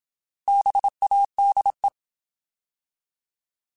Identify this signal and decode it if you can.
Morse code, heard as a keyed tone, about 27 wpm, 800 Hz. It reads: BADE
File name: Alphabetical / Alphabetical mp3s / Bs @25